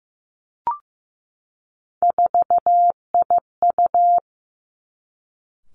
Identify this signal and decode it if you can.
Morse code, heard as a keyed tone, about 15 wpm, 700 Hz. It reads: 4IU
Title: Three Challenging Letter-Number Combinations 15wpm